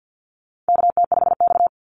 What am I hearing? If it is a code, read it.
Morse code, heard as a keyed tone, about 50 wpm, 700 Hz. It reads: KT5X
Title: NA SOTA Activator Callsigns 50wpm